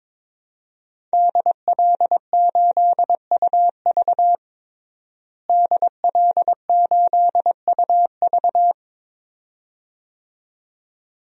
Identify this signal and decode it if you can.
Morse code, heard as a keyed tone, about 22 wpm, 700 Hz. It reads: DL8UV DL8UV
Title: DXCC-100 Call Signs - 5 Characters - Repeated 2x - 22wpm